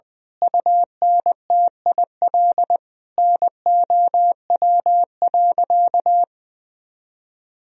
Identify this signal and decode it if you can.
Morse code, heard as a keyed tone, about 20 wpm, 700 Hz. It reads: UNTIL NOW.
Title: Sentences from Top 1000 Words - Review Part 3 - 20wpm